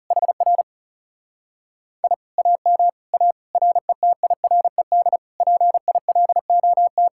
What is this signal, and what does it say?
Morse code, heard as a keyed tone, about 40 wpm, 700 Hz. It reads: HR I AM A RETIRED PILOT